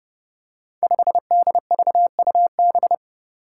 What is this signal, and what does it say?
Morse code, heard as a keyed tone, about 31 wpm, 700 Hz. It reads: 5DVUB
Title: Five Challenging Letter-Number Combinations 30wpm